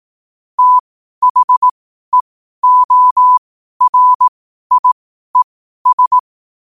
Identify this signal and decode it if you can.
Morse code, heard as a keyed tone, about 18 wpm, spaced as for 12 wpm, 1000 Hz. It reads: THEORIES
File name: qotdfiles / 12wpm